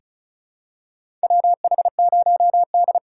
Morse code, heard as a keyed tone, about 36 wpm, 700 Hz. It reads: WH0D
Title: USA Call Signs - 4x Character Spacing - 35wpm